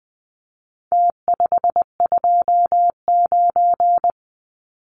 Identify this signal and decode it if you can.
Morse code, heard as a keyed tone, about 20 wpm, 700 Hz. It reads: T529
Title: Numbers - Contest Style - 4-Digit 20wpm